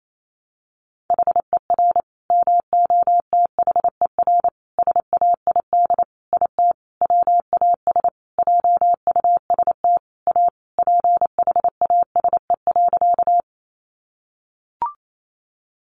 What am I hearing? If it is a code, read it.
Morse code, heard as a keyed tone, about 28 wpm, 700 Hz. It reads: HER MOTHER SAID IT WAS JUST A PHASE.
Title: Sentences from Top 1200 Words 28wpm